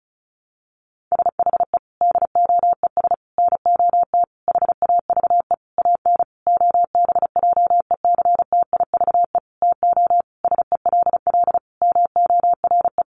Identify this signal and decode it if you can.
Morse code, heard as a keyed tone, about 34 wpm, 700 Hz. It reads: SHE DOES NOT HAVE AN OBJECTIVE TO SELL MORE
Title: Sentences from Top 1200 Words 35wpm